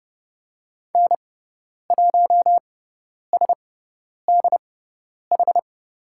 Morse code, heard as a keyed tone, about 30 wpm, spaced as for 10 wpm, 700 Hz. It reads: N1SDH
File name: Lesson 17g - Letter u - Call Signs 30wpm